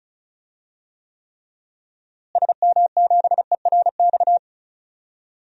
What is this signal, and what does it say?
Morse code, heard as a keyed tone, about 36 wpm, 700 Hz. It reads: SM7ERX
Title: DXCC-100 Call Signs - 6 Characters - Repeated 3x - 35wpm